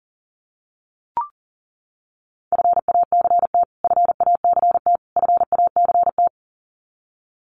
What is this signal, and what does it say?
Morse code, heard as a keyed tone, about 40 wpm, 700 Hz. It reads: FACT FACT FACT
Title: Top 500 Words Repeated 40wpm